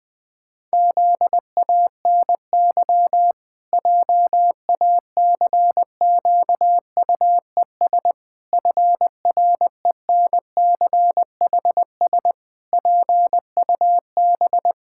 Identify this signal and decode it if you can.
Morse code, heard as a keyed tone, about 20 wpm, 700 Hz. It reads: ZANY JACQUES FRENCHS PUB
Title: Pangram Challenge v2 - 20wpm